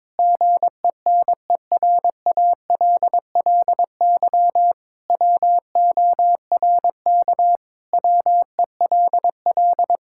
Morse code, heard as a keyed tone, about 22 wpm, 700 Hz. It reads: GENERALLY WORK WELL
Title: Sentences from Top 1000 Words - Review Part 6 - 22wpm